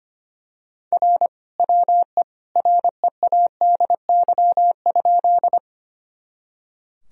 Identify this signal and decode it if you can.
Morse code, heard as a keyed tone, about 25 wpm, 700 Hz. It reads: R WE READY?
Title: Sentences from Top 400 Words and Abbreviations 25wpm